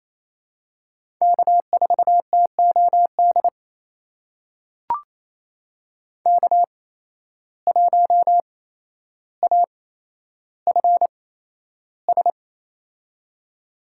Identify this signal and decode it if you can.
Morse code, heard as a keyed tone, about 28 wpm, spaced as for 12 wpm, 700 Hz. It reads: K4TOD K1AFS
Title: USA Call Signs - 5 Characters - 8x Character Spacing - 28wpm